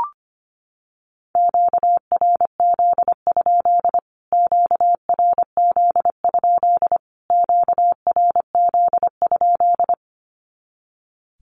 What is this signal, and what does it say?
Morse code, heard as a keyed tone, about 25 wpm, 700 Hz. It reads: QRZ? QRZ? QRZ?
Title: Q-Codes Repeated 3x 25wpm